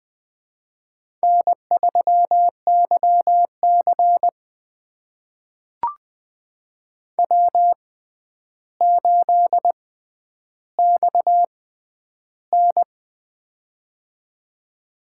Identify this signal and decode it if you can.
Morse code, heard as a keyed tone, about 20 wpm, spaced as for 10 wpm, 700 Hz. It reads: N3YC W8XN
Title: USA Call Signs - 4 Characters - 6x Character Spacing - 20wpm